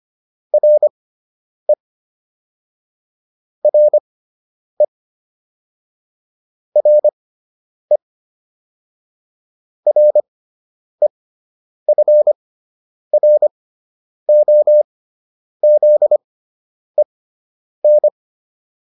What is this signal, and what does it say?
Morse code, heard as a keyed tone, about 25 wpm, spaced as for 9 wpm, 600 Hz. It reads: RE RE RE REFROZEN